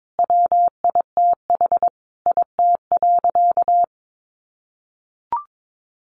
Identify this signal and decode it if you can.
Morse code, heard as a keyed tone, about 22 wpm, 700 Hz. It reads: WITH IT.
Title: Sentences from Top 800 Words 22wpm